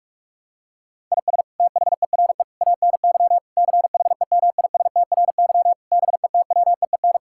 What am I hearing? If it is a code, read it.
Morse code, heard as a keyed tone, about 45 wpm, 700 Hz. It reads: IS THERE ANY CHEMISTRY BETWEEN
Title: Sentences from Top 1000 Words - Review Part 6 - 45wpm